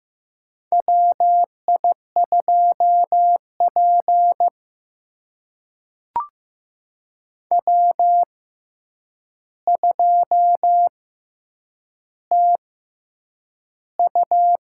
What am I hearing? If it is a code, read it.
Morse code, heard as a keyed tone, about 15 wpm, spaced as for 8 wpm, 700 Hz. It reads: WI2P W2TU
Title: USA Call Signs - 4 Characters - 6x Character Spacing - 15wpm